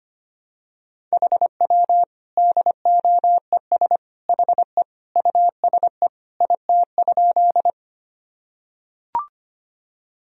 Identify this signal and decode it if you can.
Morse code, heard as a keyed tone, about 25 wpm, 700 Hz. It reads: HW DOES HE USE IT?
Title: Sentences from Top 200 Words and Abbreviations 25wpm